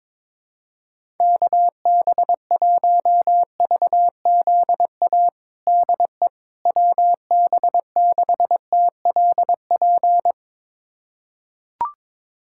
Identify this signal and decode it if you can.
Morse code, heard as a keyed tone, about 22 wpm, 700 Hz. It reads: KB1VZA DE WB6TLP
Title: USA Callsign DE Callsign - 6 Characters - 22wpm